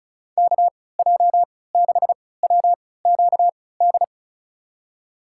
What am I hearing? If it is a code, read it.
Morse code, heard as a keyed tone, about 36 wpm, spaced as for 20 wpm, 700 Hz. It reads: KJ6WQD
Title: USA Call Signs - 6 Characters - 3x Character Spacing - 35wpm